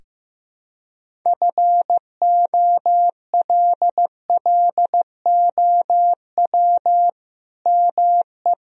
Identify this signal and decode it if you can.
Morse code, heard as a keyed tone, about 15 wpm, 700 Hz. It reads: FOLLOW ME
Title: Sentences from Top 1000 Words - Review Part 1 - 15wpm